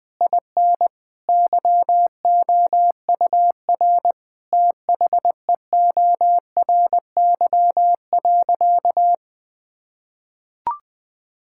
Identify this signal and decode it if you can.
Morse code, heard as a keyed tone, about 20 wpm, 700 Hz. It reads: IN YOUR THEORY.
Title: Sentences from Top 1000 Words - Review Part 5 - 20wpm